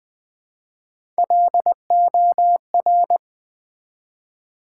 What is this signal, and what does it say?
Morse code, heard as a keyed tone, about 20 wpm, 700 Hz. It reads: LOR
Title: Build A Word - 1 Letter At A Time - 3 to 5-Letter Words - 20wpm